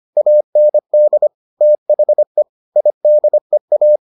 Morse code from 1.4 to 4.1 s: THE IDEA

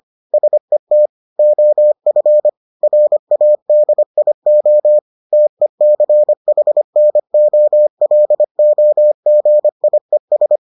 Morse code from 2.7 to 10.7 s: RADIO TECHNOLOGIES

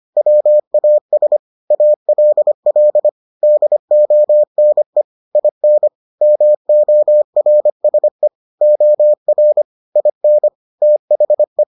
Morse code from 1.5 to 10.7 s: ALL DONE IN MORSE OR IN

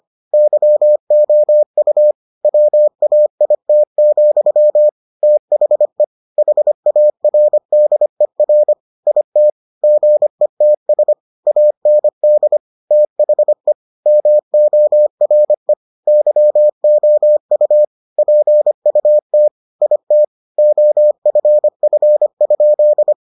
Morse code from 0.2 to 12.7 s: YOU WAIT, THE HARDER IT GETS AND

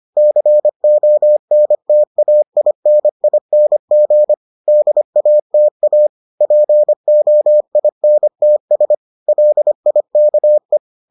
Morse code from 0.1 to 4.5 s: CONTAINING

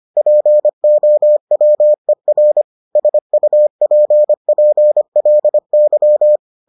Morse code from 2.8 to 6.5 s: SUPPLY

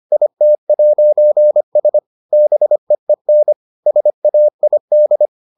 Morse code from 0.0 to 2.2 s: IT'S

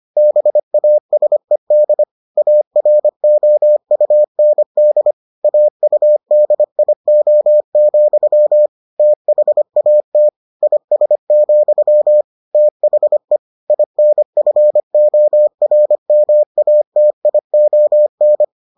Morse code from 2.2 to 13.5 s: AROUND AUDIO, THAT IS, THE

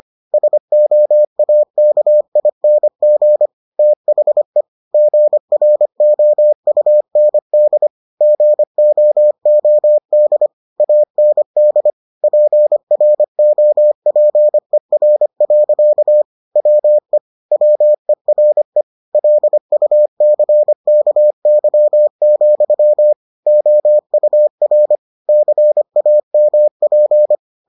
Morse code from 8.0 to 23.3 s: GOOD AND PROPER. WE WERE LUCKY,

